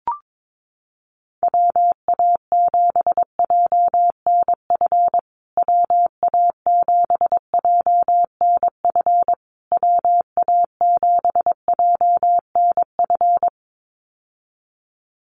Morse code, heard as a keyed tone, about 22 wpm, 700 Hz. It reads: WA7JNF WA7JNF WA7JNF
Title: USA Call Signs Repeated 3x 22wpm